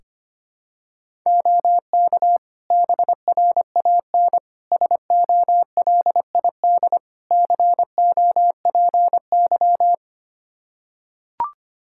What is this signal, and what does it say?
Morse code, heard as a keyed tone, about 25 wpm, 700 Hz. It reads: OK BRAN SOLID COPY
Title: Practice QSO - Exchange 3 - More Info - 25wpm